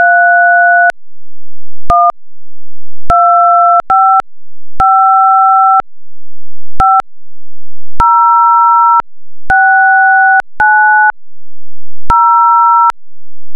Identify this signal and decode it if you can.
DTMF, heard as telephone touch tones, tones of about 780 ms. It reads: 3 1 255 5 069 0
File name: 16 Audio / 02 Phone Digits / touch-tone-312-555-0690